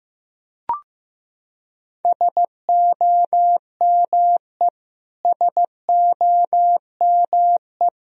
Morse code, heard as a keyed tone, about 15 wpm, 700 Hz. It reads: SOME SOME
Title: Top 100 Words Repeated 15wpm